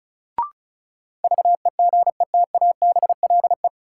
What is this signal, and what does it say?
Morse code, heard as a keyed tone, about 36 wpm, 700 Hz. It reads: VEGETABLE
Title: Sentences from Top 1300 Words - New Word Drill Alt - 35wpm